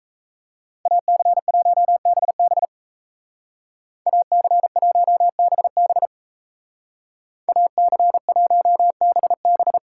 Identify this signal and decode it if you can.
Morse code, heard as a keyed tone, about 33 wpm, 700 Hz. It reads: AC1BB AC1BB AC1BB
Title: Speed Racing - USA 2x2 Call Signs 28wpm